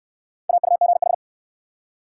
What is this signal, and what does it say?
Morse code, heard as a keyed tone, about 57 wpm, 700 Hz. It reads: AUGH